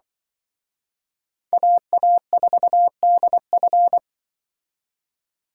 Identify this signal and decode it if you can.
Morse code, heard as a keyed tone, about 24 wpm, 700 Hz. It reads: AA4DF
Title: Speed Racing - USA 2x2 Call Signs 20wpm